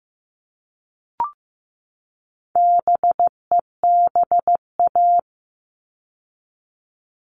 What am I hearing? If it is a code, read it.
Morse code, heard as a keyed tone, about 15 wpm, 700 Hz. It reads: BEBA
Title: All Cut Numbers - 4-Digit 15wpm